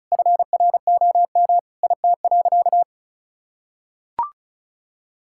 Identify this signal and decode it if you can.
Morse code, heard as a keyed tone, about 34 wpm, 700 Hz. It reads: FROM IT.
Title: Sentences from Top 1000 Words - Review Part 5 - 35wpm